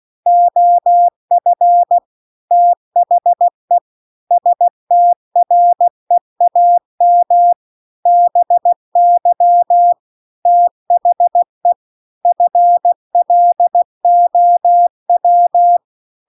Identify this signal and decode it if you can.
Morse code, heard as a keyed tone, about 16 wpm, 700 Hz. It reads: OF THE STREAM BY THE FLOW
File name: CW-The-Big-Twoheaded-River-2-16WPM